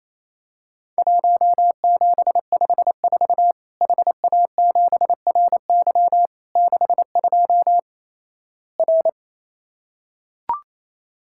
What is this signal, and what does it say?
Morse code, heard as a keyed tone, about 28 wpm, 700 Hz. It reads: 1754 HA7RY 62 R
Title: Worked All Europe DX Contest - QTC Exchange - 28wpm